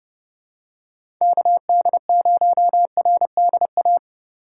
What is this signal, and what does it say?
Morse code, heard as a keyed tone, about 30 wpm, 700 Hz. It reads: KD0RDA
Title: USA Call Signs - 6 Characters - 3x Character Spacing - 30wpm